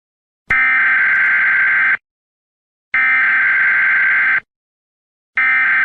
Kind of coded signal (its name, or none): Emergency Alert System (SAME) tones